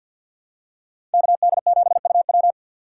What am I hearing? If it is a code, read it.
Morse code, heard as a keyed tone, about 53 wpm, 700 Hz. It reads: KD7UW
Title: USA Call Signs - 5 Characters - 4x Character Spacing - 50wpm